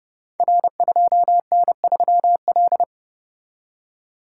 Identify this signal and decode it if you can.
Morse code, heard as a keyed tone, about 29 wpm, 700 Hz. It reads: R2N3L
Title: Code Groups - 5 Letter:Numbers - 30wpm